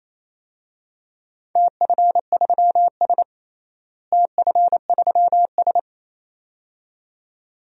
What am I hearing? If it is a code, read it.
Morse code, heard as a keyed tone, about 28 wpm, 700 Hz. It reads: TF3S TF3S
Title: DXCC-100 Call Signs - 4 Characters - Repeated 2x - 28wpm